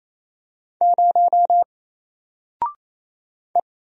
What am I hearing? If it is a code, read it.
Morse code, heard as a keyed tone, about 27 wpm, 700 Hz. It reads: 0 E
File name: Single-Letter-Number-28wpm